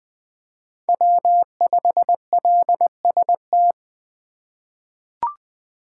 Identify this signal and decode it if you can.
Morse code, heard as a keyed tone, about 20 wpm, 700 Hz. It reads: W5LST
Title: 2023 POTA Activator Call Signs - 20wpm